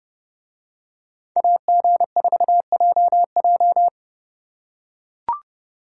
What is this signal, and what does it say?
Morse code, heard as a keyed tone, about 30 wpm, 700 Hz. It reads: AG4JJ